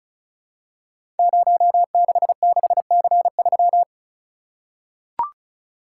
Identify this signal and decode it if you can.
Morse code, heard as a keyed tone, about 36 wpm, 700 Hz. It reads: 066C3
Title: Code Groups - 5 Letter:Numbers - 35wpm